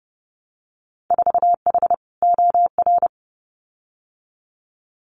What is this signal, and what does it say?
Morse code, heard as a keyed tone, about 30 wpm, 700 Hz. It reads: 4H OR
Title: Winter Field Day Exchanges Encore 30wpm